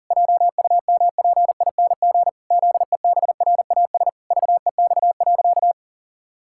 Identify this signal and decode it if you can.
Morse code, heard as a keyed tone, about 40 wpm, 700 Hz. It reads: JUMPING ZEBRAS VEX.
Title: Pangram Challenge v2 - 40wpm